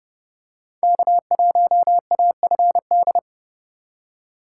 Morse code, heard as a keyed tone, about 30 wpm, 700 Hz. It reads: K1AFD
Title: USA Call Signs - 5 Characters - 8x Character Spacing - 30wpm